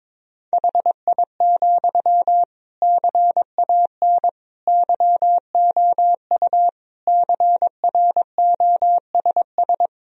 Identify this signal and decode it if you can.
Morse code, heard as a keyed tone, about 22 wpm, 700 Hz. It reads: HI, CAN YOU CROSS
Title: Sentences from Top 1000 Words - Review Part 3 - 22wpm